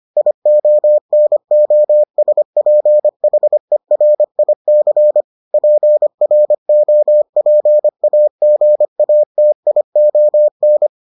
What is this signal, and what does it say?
Morse code, heard as a keyed tone, about 25 wpm, 600 Hz. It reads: IONOSPHERIC PROPAGATION